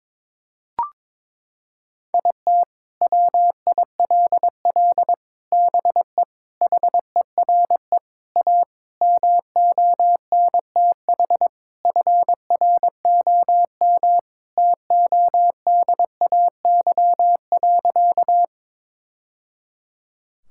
Morse code, heard as a keyed tone, about 22 wpm, 700 Hz. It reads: IT WILL BE HERE A MONTH FROM TODAY.
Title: Sentences from Top 1000 Words - Review Part 5 - 22wpm